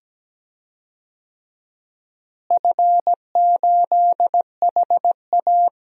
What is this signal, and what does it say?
Morse code, heard as a keyed tone, about 17 wpm, 700 Hz. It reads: F8HA
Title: DXCC-100 Call Signs - 4 Characters - Repeated 2x - 17wpm